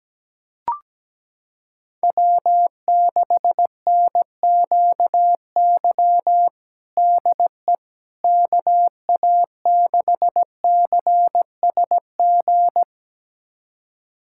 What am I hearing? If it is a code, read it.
Morse code, heard as a keyed tone, about 17 wpm, 700 Hz. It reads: W6NQY DE KA6CSG
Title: USA Callsign DE Callsign 17wpm